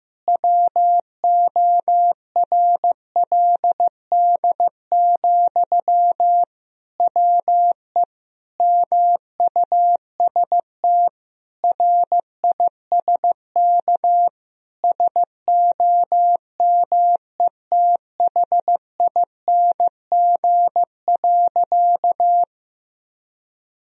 Morse code, heard as a keyed tone, about 15 wpm, 700 Hz. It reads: WORLD, WE MUST RISK SOMETHING.
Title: Sentences from Top 1000 Words - Review Part 4 - 15wpm